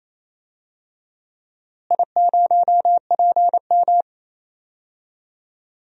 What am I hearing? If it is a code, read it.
Morse code, heard as a keyed tone, about 28 wpm, 700 Hz. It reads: I0PM
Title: DXCC-100 Call Signs - 4 Characters - Repeated 2x - 28wpm